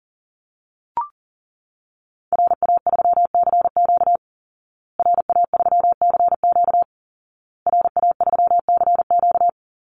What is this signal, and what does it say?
Morse code, heard as a keyed tone, about 37 wpm, 700 Hz. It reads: RA3CQ RA3CQ RA3CQ
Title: Call Signs Repeated 2x - 40wpm